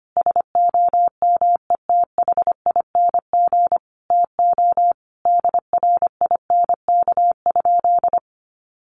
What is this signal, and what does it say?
Morse code, heard as a keyed tone, about 25 wpm, 700 Hz. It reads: SOMETHING TO DRINK?
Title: Sentences from Top 1500 Words 25wpm